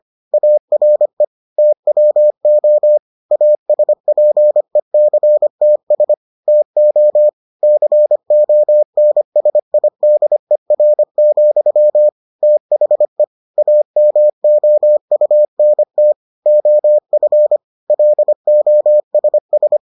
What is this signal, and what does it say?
Morse code, heard as a keyed tone, about 25 wpm, 600 Hz. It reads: ARE TWO ASPECTS TO CONSIDER, THE AMOUNT OF LOSS